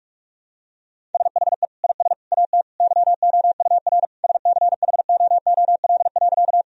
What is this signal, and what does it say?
Morse code, heard as a keyed tone, about 46 wpm, 700 Hz. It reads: SHE IS AT YOUR SCHOOL.
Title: Sentences from Top 1000 Words - Review Part 1 - 45wpm